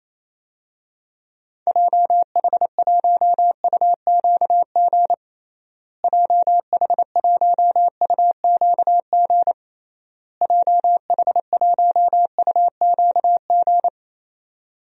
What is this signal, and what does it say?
Morse code, heard as a keyed tone, about 27 wpm, 700 Hz. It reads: JH1UQG JH1UQG JH1UQG
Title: DXCC-100 Call Signs - 6 Characters - Repeated 3x - 28wpm